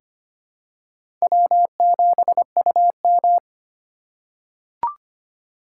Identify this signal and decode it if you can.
Morse code, heard as a keyed tone, about 25 wpm, 700 Hz. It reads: W7UM
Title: USA Call Signs - 4 Characters - 4x Character Spacing - 25wpm